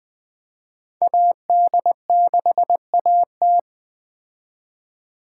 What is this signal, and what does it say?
Morse code, heard as a keyed tone, about 21 wpm, 700 Hz. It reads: AD6AT
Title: Speed Racing - USA 2x2 Call Signs 17wpm